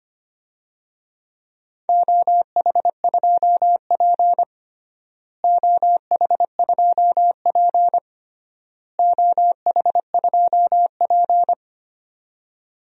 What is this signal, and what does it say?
Morse code, heard as a keyed tone, about 25 wpm, 700 Hz. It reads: OH2P OH2P OH2P
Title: DXCC-100 Call Signs - 4 Characters - Repeated 3x - 25wpm